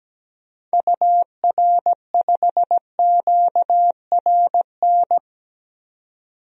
Morse code, heard as a keyed tone, about 17 wpm, 700 Hz. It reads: UR5QRN
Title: DXCC-100 Call Signs - 6 Characters - 17wpm